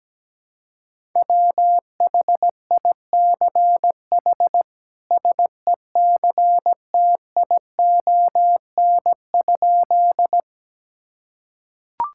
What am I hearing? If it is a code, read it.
Morse code, heard as a keyed tone, about 17 wpm, 700 Hz. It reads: WHICH SECTION?
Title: Sentences from Top 1000 Words - Review Part 5 - 17wpm